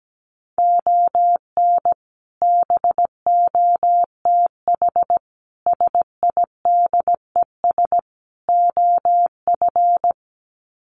Morse code, heard as a keyed tone, about 17 wpm, 700 Hz. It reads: ON BOTH SIDES OF
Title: Sets of 4 Words 17wpm